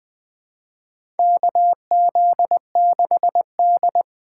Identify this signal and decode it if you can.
Morse code, heard as a keyed tone, about 20 wpm, 700 Hz. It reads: KZ6D
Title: USA Call Signs - 4 Characters - 3x Character Spacing - 20wpm